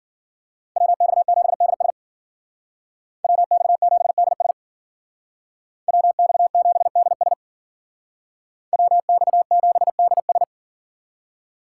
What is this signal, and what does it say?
Morse code, heard as a keyed tone, about 49 wpm, 700 Hz. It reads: WX7DS WX7DS WX7DS WX7DS
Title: Speed Racing - USA 2x2 Call Signs 40wpm